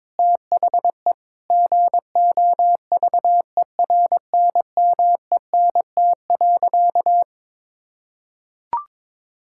Morse code, heard as a keyed tone, about 22 wpm, 700 Hz. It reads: THE GOVERNMENT.
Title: Sentences from Top 1000 Words - Review Part 6 - 22wpm